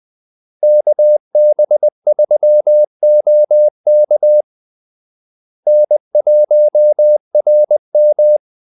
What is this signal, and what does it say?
Morse code, heard as a keyed tone, about 20 wpm, 600 Hz. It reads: KB3OK N1RM
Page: Kaldesignaler 20 wpm